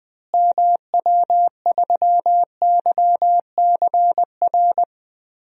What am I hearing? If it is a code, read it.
Morse code, heard as a keyed tone, about 20 wpm, 700 Hz. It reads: MW3YCR
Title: DXCC-100 Call Signs - 6 Characters - Repeated 3x - 20wpm